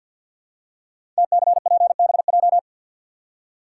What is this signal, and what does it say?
Morse code, heard as a keyed tone, about 52 wpm, 700 Hz. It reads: TCPBJ
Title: Code Groups - 5 Letters - 50wpm